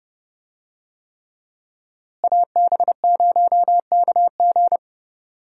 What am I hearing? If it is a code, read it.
Morse code, heard as a keyed tone, about 30 wpm, 700 Hz. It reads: AB0KG
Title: USA Call Signs - 5 Characters - Repeated 3x - 30wpm